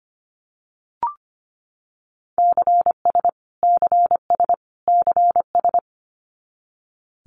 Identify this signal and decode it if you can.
Morse code, heard as a keyed tone, about 25 wpm, 700 Hz. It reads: CS CS CS
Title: Abbreviations Repeated 3x - 25wpm